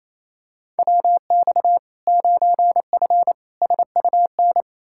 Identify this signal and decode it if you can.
Morse code, heard as a keyed tone, about 28 wpm, 700 Hz. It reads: WX 9F SUN
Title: Practice QSO - Exchange 2 - Info - 28wpm